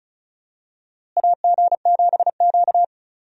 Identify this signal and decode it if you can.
Morse code, heard as a keyed tone, about 36 wpm, 700 Hz. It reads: AG7Q
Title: USA Call Signs - 4 Characters - Repeated 2x - 35wpm